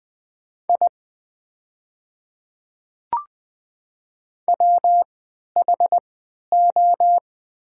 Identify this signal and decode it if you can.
Morse code, heard as a keyed tone, about 20 wpm, spaced as for 11 wpm, 700 Hz. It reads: I WHO
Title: 3x Character Spacing - Top 100 Words - 20wpm